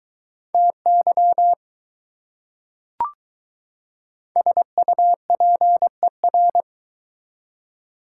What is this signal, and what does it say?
Morse code, heard as a keyed tone, about 21 wpm, 700 Hz. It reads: TY SUPER